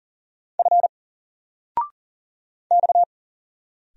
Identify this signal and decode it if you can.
Morse code, heard as a keyed tone, about 40 wpm, 700 Hz. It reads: F X
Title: Single Letter-Number 40wpm